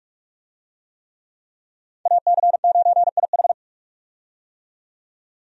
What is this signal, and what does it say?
Morse code, heard as a keyed tone, about 44 wpm, 700 Hz. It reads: AC9IH
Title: USA Call Signs - 5 Characters - 45wpm